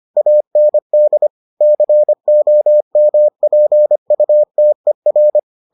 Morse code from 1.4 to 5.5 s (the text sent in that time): COMPUTER